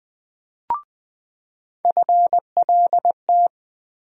Morse code, heard as a keyed tone, about 21 wpm, 700 Hz. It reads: FLT
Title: USA Call Sign Postfixes - 3 Letters 20wpm